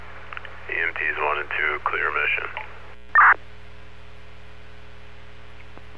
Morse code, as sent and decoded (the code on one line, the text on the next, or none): none